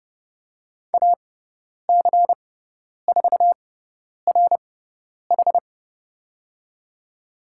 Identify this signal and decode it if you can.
Morse code, heard as a keyed tone, about 30 wpm, spaced as for 10 wpm, 700 Hz. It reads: AC4RH
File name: Lesson 17g - Letter u - Call Signs 30wpm